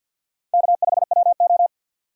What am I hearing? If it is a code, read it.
Morse code, heard as a keyed tone, about 52 wpm, 690 Hz. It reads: K5WO
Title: USA Call Signs - 4x Character Spacing - 50wpm